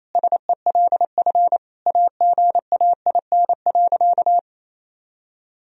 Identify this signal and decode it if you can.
Morse code, heard as a keyed tone, about 28 wpm, 700 Hz. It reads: SELF AGAIN.
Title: Sentences from Top 1000 Words - Review Part 1 - 28wpm